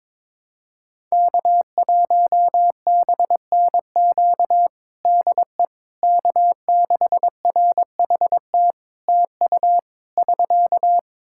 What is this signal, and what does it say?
Morse code, heard as a keyed tone, about 22 wpm, 700 Hz. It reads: K1BNQ DE K6RHT TU <SK>
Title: K1USN SST QSO Practice 22wpm